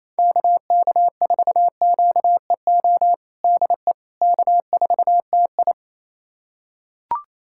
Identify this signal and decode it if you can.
Morse code, heard as a keyed tone, about 28 wpm, 700 Hz. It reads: KK4QEO DE K4TI
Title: USA Callsign DE Callsign 28wpm